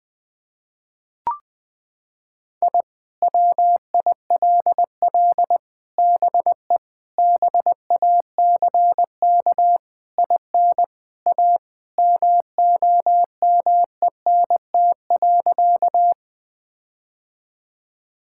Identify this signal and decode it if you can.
Morse code, heard as a keyed tone, about 21 wpm, 700 Hz. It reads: I WILL BE BACK IN A MOMENT.
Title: Sentences with 5-Word Phrases - 20wpm